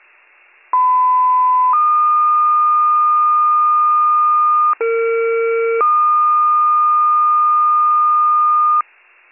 Two-tone page, heard as a one-second tone, then a three-second tone, about 1010 then 1220 Hz, about 460 then 1220 Hz. It is heard from 0.7 to 8.8 s.